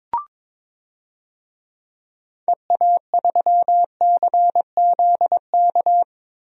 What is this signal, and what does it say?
Morse code, heard as a keyed tone, about 21 wpm, 700 Hz. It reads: EA3CZK
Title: DXCC-100 Call Signs - 6 Characters - Repeated 2x - 22wpm